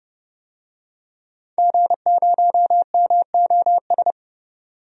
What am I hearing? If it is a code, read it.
Morse code, heard as a keyed tone, about 30 wpm, 700 Hz. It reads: G0MOS
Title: DXCC-100 Call Signs - 5 Characters - 30wpm